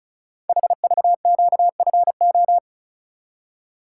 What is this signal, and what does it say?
Morse code, heard as a keyed tone, about 36 wpm, 700 Hz. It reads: HVQFO
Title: Code Groups - 5 Letters - 35wpm